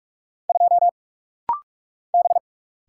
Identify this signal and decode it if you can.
Morse code, heard as a keyed tone, about 44 wpm, 700 Hz. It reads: 2 B
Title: Single Letter-Number Alt - Rapid-Fire 45wpm